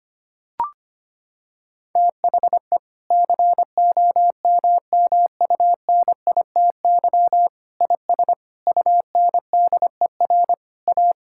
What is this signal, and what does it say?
Morse code, heard as a keyed tone, about 25 wpm, 700 Hz. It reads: THE COMMUNITY IS UNDER A
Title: Sentences from Top 1000 Words - Review Part 6 - 25wpm